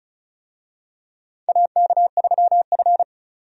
Morse code, heard as a keyed tone, about 34 wpm, 700 Hz. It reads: AK3F
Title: USA Call Signs - 4 Characters - 35wpm